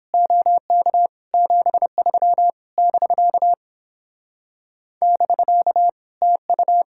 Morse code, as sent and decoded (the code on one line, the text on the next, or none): --- -.- / --... ...-- / -...-.- / -...-.- / - ..-
OK 73 <BK> <BK> TU